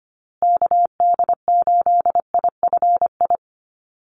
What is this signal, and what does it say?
Morse code, heard as a keyed tone, about 25 wpm, 700 Hz. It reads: KD8IFI
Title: USA Call Signs Repeated 2x 25wpm